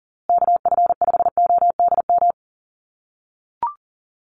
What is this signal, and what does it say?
Morse code, heard as a keyed tone, about 40 wpm, 700 Hz. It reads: KF5ODM
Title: USA Call Signs Repeated 3x 40wpm